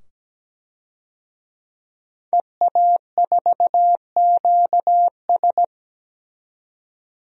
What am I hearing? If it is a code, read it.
Morse code, heard as a keyed tone, about 17 wpm, 700 Hz. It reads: EA4QS
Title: DXCC-100 Call Signs - 5 Characters - 17wpm